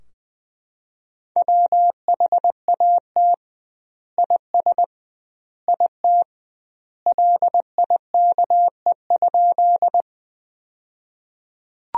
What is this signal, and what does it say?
Morse code, heard as a keyed tone, about 21 wpm, 700 Hz. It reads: WHAT IS IT LIKE?
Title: 2x Word-Spacing - Sentences from Top 100 Words - 20wpm